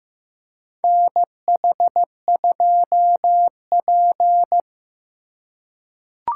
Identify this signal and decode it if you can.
Morse code, heard as a keyed tone, about 15 wpm, 700 Hz. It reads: NH2P
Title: USA Call Signs - 4 Characters - 3x Character Spacing - 15wpm